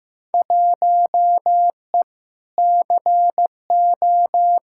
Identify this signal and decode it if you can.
Morse code, heard as a keyed tone, about 15 wpm, 700 Hz. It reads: 1E CO
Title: Field Day Exchanges - 15wpm